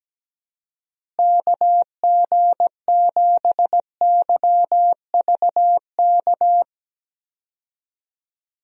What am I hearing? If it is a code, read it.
Morse code, heard as a keyed tone, about 17 wpm, 700 Hz. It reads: KG7YVK
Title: USA Call Signs - 6 Characters - Repeated 2x - 17wpm